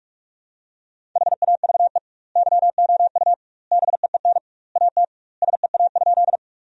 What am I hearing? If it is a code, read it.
Morse code, heard as a keyed tone, about 46 wpm, 700 Hz. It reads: HAVE YOU BEEN AT SEA?
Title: 2x Word-Spacing - Sentences from Top 300 Words - 45wpm